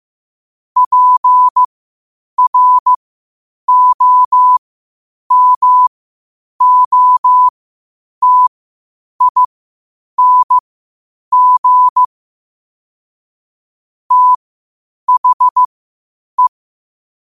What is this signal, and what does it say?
Morse code, heard as a keyed tone, about 15 wpm, spaced as for 8 wpm, 1000 Hz. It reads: PROMOTING THE
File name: qotdfiles / 8wpm